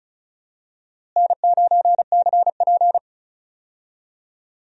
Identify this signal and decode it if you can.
Morse code, heard as a keyed tone, about 34 wpm, 700 Hz. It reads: N9CP
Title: USA Call Signs - 4 Characters - Repeated 2x - 35wpm